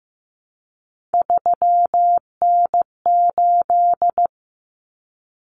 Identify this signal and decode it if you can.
Morse code, heard as a keyed tone, about 15 wpm, 700 Hz. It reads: 3N8
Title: Common Cut Numbers - 3-Digit 15wpm